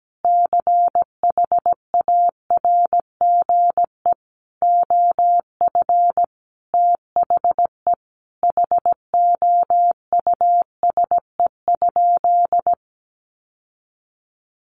Morse code, heard as a keyed tone, about 17 wpm, 700 Hz. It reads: CHARGE OF THE HOUSE?
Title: Sentences from Top 900 Words 17wpm